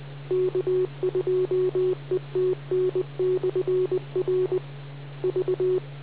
Morse code, heard as a keyed tone, about 20 wpm, 370 Hz. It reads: K2ETN/R V